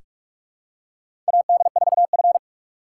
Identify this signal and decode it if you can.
Morse code, heard as a keyed tone, about 46 wpm, 700 Hz. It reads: AD4F
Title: USA Call Signs - 3x Character Spacing - 45wpm